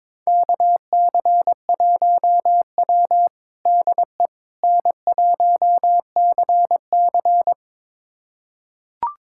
Morse code, heard as a keyed tone, about 22 wpm, 700 Hz. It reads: KC1W DE N1CC
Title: USA Callsign DE Callsign - 4 Characters - 22wpm